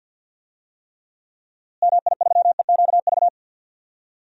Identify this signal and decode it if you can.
Morse code, heard as a keyed tone, about 46 wpm, 700 Hz. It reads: MI3EQV
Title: DXCC-100 Call Signs - 6 Characters - Repeated 3x - 50wpm